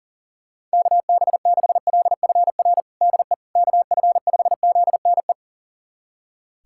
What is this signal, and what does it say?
Morse code, heard as a keyed tone, about 37 wpm, 700 Hz. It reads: KB6LFR DE KF5ZNE
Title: USA Callsign DE Callsign 40wpm